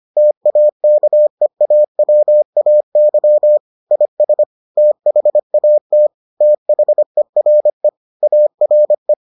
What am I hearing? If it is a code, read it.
Morse code, heard as a keyed tone, about 25 wpm, 600 Hz. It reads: TAKEAWAY IS THAT THERE ARE